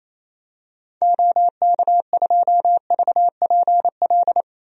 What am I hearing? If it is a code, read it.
Morse code, heard as a keyed tone, about 28 wpm, 700 Hz. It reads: OK2VPL
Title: DXCC-100 Call Signs - 6 Characters - Repeated 2x - 28wpm